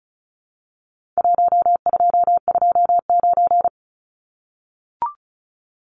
Morse code, heard as a keyed tone, about 34 wpm, 700 Hz. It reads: 1229
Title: Numbers - Contest Style - 4-Digit 35wpm